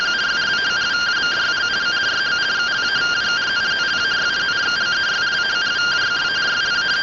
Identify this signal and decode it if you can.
Two-tone teleprinter signal, about 170 Hz shift.